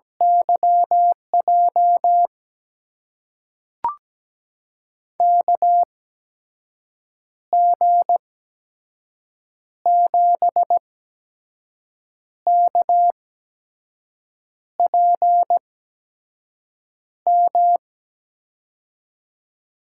Morse code, heard as a keyed tone, about 17 wpm, spaced as for 5 wpm, 700 Hz. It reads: YJ KG7KPM
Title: USA Call Signs - 6 Characters - 8x Character Spacing - 17wpm